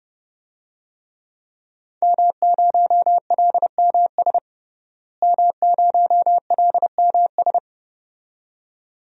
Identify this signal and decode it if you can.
Morse code, heard as a keyed tone, about 30 wpm, 700 Hz. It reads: M0LMS M0LMS
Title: DXCC-100 Call Signs - 5 Characters - Repeated 2x - 30wpm